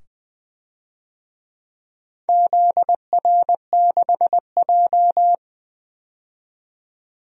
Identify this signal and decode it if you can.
Morse code, heard as a keyed tone, about 21 wpm, 700 Hz. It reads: ZR6J
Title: DXCC-100 Call Signs - 4 Characters - 20wpm